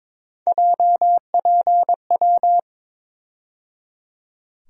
Morse code, heard as a keyed tone, about 22 wpm, 700 Hz. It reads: JPW
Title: USA Call Sign Postfixes - 3 Letters 22wpm